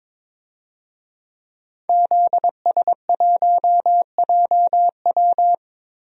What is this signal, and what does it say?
Morse code, heard as a keyed tone, about 22 wpm, 700 Hz. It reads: ZS1JW
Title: DXCC-100 Call Signs - 5 Characters - Repeated 3x - 22wpm